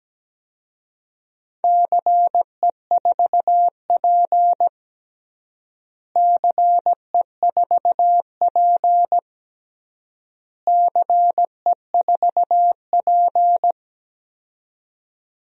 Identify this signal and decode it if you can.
Morse code, heard as a keyed tone, about 17 wpm, 700 Hz. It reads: CE4P CE4P CE4P
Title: DXCC-100 Call Signs - 4 Characters - Repeated 3x - 17wpm